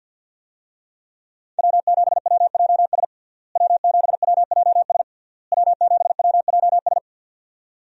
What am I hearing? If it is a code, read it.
Morse code, heard as a keyed tone, about 48 wpm, 700 Hz. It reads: W7WJS W7WJS W7WJS
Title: USA Call Signs - 5 Characters - Repeated 3x - 50wpm